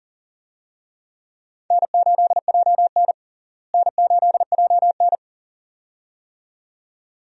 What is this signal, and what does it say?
Morse code, heard as a keyed tone, about 40 wpm, 700 Hz. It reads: N8JN N8JN
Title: USA Call Signs - 4 Characters - Repeated 2x - 40wpm